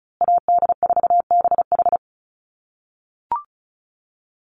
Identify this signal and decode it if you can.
Morse code, heard as a keyed tone, about 34 wpm, 700 Hz. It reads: AD4BH
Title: USA Call Signs Repeated 2x 35wpm